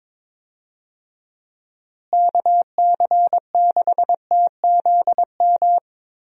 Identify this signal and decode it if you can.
Morse code, heard as a keyed tone, about 22 wpm, 700 Hz. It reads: KC6TZM
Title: USA Call Signs - 6 Characters - Repeated 2x - 22wpm